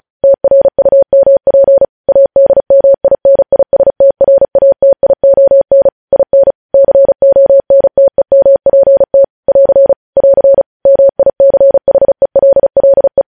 Morse code, heard as a keyed tone, about 36 wpm, 550 Hz. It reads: TRUMP ADMINISTRATION IN CONTEMPT <AR> <AR> MICHELLE
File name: podcast / News headlines-1x-35x35